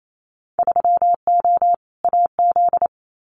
Morse code, heard as a keyed tone, about 28 wpm, 700 Hz. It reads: 3O AZ
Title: Winter Field Day Exchanges 28wpm